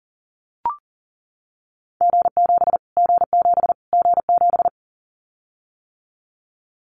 Morse code, heard as a keyed tone, about 40 wpm, 700 Hz. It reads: G7 G7 G7
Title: Callsign Prefixes for Top 25 DXCC - Repeated 3x - 40wpm